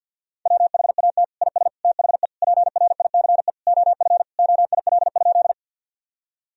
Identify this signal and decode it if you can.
Morse code, heard as a keyed tone, about 50 wpm, 700 Hz. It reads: WHAT IS THE PRICE OF OIL?
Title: Sentences from Top 1000 Words - Review Part 4 - 50wpm